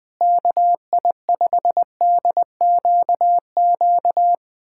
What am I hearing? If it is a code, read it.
Morse code, heard as a keyed tone, about 20 wpm, 700 Hz. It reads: KI5DQQ
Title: USA Callsign DE Callsign - 6 Characters - 20wpm